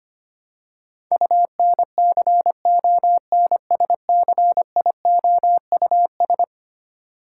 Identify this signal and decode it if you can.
Morse code, heard as a keyed tone, about 25 wpm, 700 Hz. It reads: UNCONSCIOUS
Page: morse-code-ninja-practice-25wpm